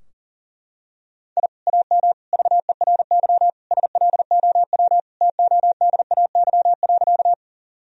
Morse code, heard as a keyed tone, about 40 wpm, 700 Hz. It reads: I AM VERY SLOW TODAY.
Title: Sentences from Top 1000 Words - Review Part 2 - 40wpm